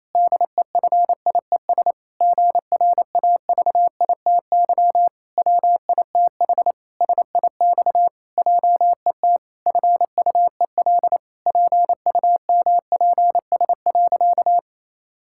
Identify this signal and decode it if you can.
Morse code, heard as a keyed tone, about 28 wpm, 700 Hz. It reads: DEFIES GRAVITY WITH SIX JET FUEL PUMPS.
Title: Pangram Challenge v2 - 28wpm